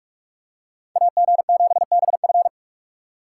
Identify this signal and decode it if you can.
Morse code, heard as a keyed tone, about 44 wpm, 700 Hz. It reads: AG7BF